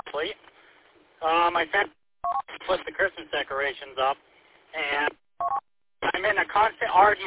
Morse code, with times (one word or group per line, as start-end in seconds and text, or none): none